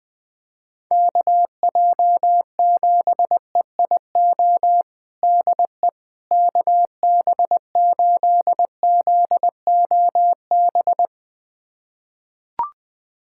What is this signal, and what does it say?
Morse code, heard as a keyed tone, about 20 wpm, 700 Hz. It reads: KJ7EIO DE KB8ZOB
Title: USA Callsign DE Callsign 20wpm